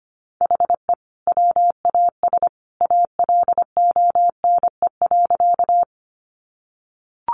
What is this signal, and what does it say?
Morse code, heard as a keyed tone, about 25 wpm, 700 Hz. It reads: HE WAS ALONE.
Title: Sentences from Top 1000 Words 25wpm